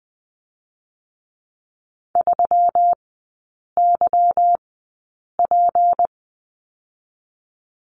Morse code, heard as a keyed tone, about 20 wpm, spaced as for 8 wpm, 700 Hz. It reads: 3YP
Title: 152 Lesson 25e - Letter p - Three Characters 20wpm